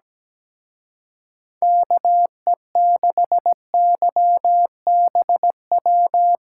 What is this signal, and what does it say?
Morse code, heard as a keyed tone, about 19 wpm, 700 Hz. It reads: KE6YBW
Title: Speed Racing - USA 2x3 Call Signs 17wpm